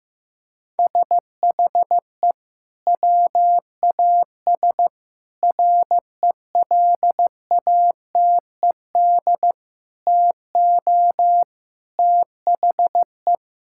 Morse code, heard as a keyed tone, about 16 wpm, 700 Hz. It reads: SHE WAS RELATED TO THE
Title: Sentences from Top 1000 Words - Review Part 5 - 15wpm